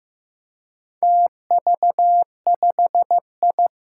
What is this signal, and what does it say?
Morse code, heard as a keyed tone, about 15 wpm, 700 Hz. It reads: TV5I
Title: Four Challenging Letter-Number Combinations 15wpm